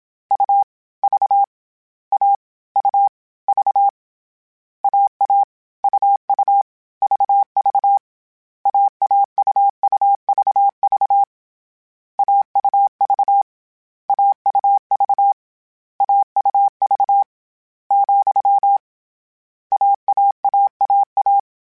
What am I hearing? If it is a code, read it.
Morse code, heard as a keyed tone, about 27 wpm, 800 Hz. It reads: U V A U V AA UU VV AAUUVV AUV AUV AUV , AAAAA
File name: Difficult Combos / Difficult Combos-mp3s / AUV AUV4 @25